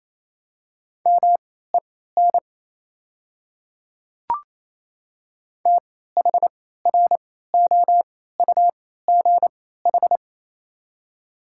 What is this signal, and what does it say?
Morse code, heard as a keyed tone, about 28 wpm, spaced as for 16 wpm, 700 Hz. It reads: MEN THROUGH
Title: 3x Character Spacing - Top 300 Words - 28wpm